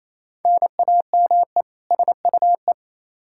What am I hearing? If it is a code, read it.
Morse code, heard as a keyed tone, about 28 wpm, 700 Hz. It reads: NAME SUE
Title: Practice QSO - Exchange 1 - Beginning - 28wpm